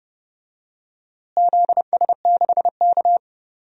Morse code, heard as a keyed tone, about 30 wpm, 700 Hz. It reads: ZS6K
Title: DXCC-100 Call Signs - 4 Characters - Repeated 2x - 30wpm